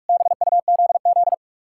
Morse code, heard as a keyed tone, about 62 wpm, spaced as for 53 wpm, 700 Hz. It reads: BUZZ
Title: Speed-Racing Common QSO Phrases Encore - 45wpm